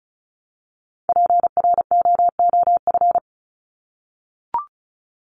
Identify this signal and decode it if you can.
Morse code, heard as a keyed tone, about 34 wpm, 700 Hz. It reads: PROOF
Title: Top 1501-1750 Words 35wpm